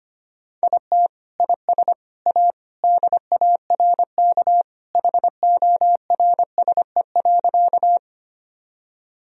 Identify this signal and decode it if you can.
Morse code, heard as a keyed tone, about 25 wpm, 700 Hz. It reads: IT IS A DARK HORSE.
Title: Sentences from Top 1000 Words - Review Part 1 - 25wpm